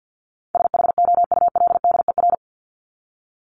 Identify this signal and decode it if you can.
Morse code, heard as a keyed tone, about 48 wpm, 700 Hz. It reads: SHOULDER
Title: Top 1501-1750 Words 50wpm